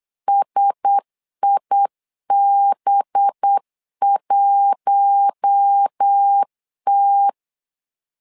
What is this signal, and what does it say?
Morse code, heard as a keyed tone, about 8 wpm, 800 Hz. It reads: SIB1T
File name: LEZIONI / 5 LBVC / 5 LB 8wpm 2816